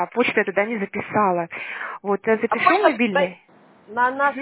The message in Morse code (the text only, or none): none